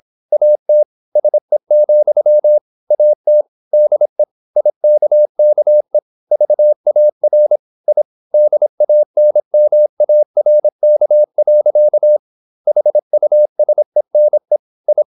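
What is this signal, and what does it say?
Morse code, heard as a keyed tone, about 26 wpm, 600 Hz. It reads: AT SE, AT DE IKKE VAR I DANMARK. HUSENE I
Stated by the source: High-speed (QRQ) Morse at 26 wpm.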